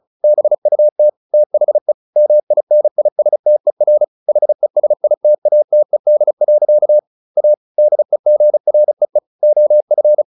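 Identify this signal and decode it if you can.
Morse code, heard as a keyed tone, about 34 wpm, 600 Hz. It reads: BUT THE MINISTER HESITATED. A DEGREE OF
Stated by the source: Morse practice at 35 wpm from CW med Gnister, sending Ken Follett's novel Never in English.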